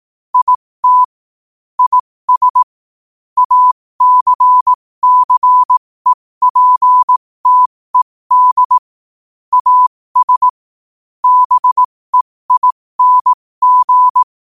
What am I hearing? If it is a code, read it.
Morse code, heard as a keyed tone, about 18 wpm, spaced as for 15 wpm, 1000 Hz. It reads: IT IS ACCEPTED AS BEING
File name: qotdfiles / 15wpm